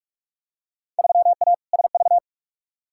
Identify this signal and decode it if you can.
Morse code, heard as a keyed tone, about 46 wpm, 700 Hz. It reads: 3A SV
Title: Field Day Exchanges - 45wpm